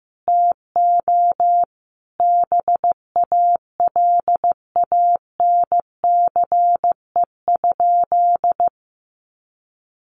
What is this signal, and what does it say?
Morse code, heard as a keyed tone, about 15 wpm, 700 Hz. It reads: TO BALANCE?
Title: Sentences from Top 900 Words 15wpm